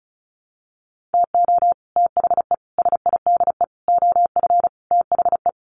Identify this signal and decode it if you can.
Morse code, heard as a keyed tone, about 34 wpm, 700 Hz. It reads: TO THE SIDE OF THE
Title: Sets of 5 Words 35wpm